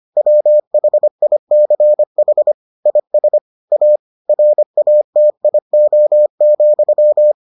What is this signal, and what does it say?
Morse code, heard as a keyed tone, about 25 wpm, 600 Hz. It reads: WHICH IS A RATIO,